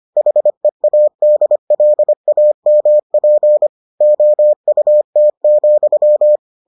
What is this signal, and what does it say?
Morse code, heard as a keyed tone, about 25 wpm, 600 Hz. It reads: HEADLAMP OUT,